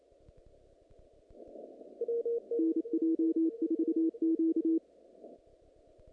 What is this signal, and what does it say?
Morse code, heard as a keyed tone, about 27 wpm, 330 Hz. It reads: NJ4Q
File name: contests / naqp-cw-k5tr-op-k5pi- -2200